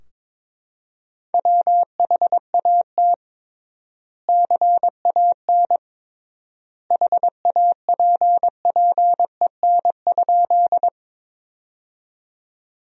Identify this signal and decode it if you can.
Morse code, heard as a keyed tone, about 23 wpm, 700 Hz. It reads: WHAT CAN HAPPEN?
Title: 3x Word-Spacing - Sentences from Top 400 Words - 22wpm